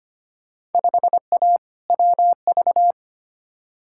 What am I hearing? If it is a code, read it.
Morse code, heard as a keyed tone, about 25 wpm, 700 Hz. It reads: 5A WV
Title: Field Day Exchanges - 25wpm